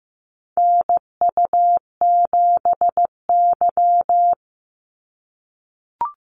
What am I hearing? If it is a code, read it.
Morse code, heard as a keyed tone, about 15 wpm, 700 Hz. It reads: NU7Y
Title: NA SOTA Activator Callsigns 15wpm